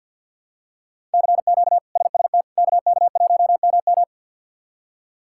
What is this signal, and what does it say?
Morse code, heard as a keyed tone, about 52 wpm, 700 Hz. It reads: CQ SST KK1MM
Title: K1USN SST QSO Practice - 50wpm